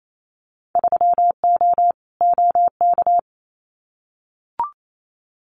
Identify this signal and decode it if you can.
Morse code, heard as a keyed tone, about 28 wpm, 700 Hz. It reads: 3O OK
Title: Winter Field Day Exchanges 28wpm